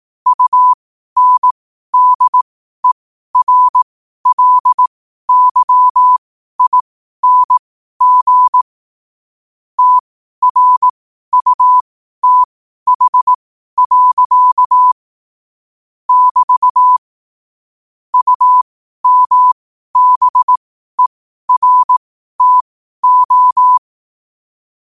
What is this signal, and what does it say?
Morse code, heard as a keyed tone, about 18 wpm, spaced as for 12 wpm, 1000 Hz. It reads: UNDERLYING TRUTH. <BT> UMBERTO
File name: qotdfiles / 12wpm